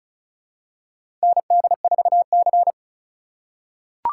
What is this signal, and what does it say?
Morse code, heard as a keyed tone, about 36 wpm, 700 Hz. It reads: ND4C
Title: USA Call Signs - 4 Characters - 4x Character Spacing - 35wpm